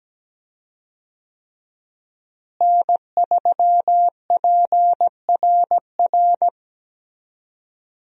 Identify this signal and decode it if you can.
Morse code, heard as a keyed tone, about 17 wpm, 700 Hz. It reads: N3PRR
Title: USA Call Signs - 5 Characters - Repeated 3x - 17wpm